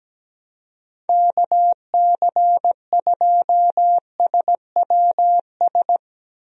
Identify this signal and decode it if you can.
Morse code, heard as a keyed tone, about 17 wpm, 700 Hz. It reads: KC2SWS
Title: USA Call Signs - 6 Characters - Repeated 3x - 17wpm